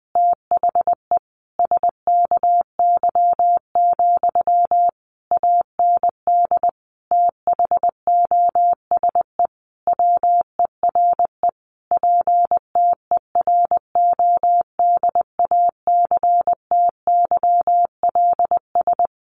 Morse code, heard as a keyed tone, about 20 wpm, 700 Hz. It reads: THE SKY, AND THOSE WERE PTERODACTYLS
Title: A Sound Of Thunder 20wpm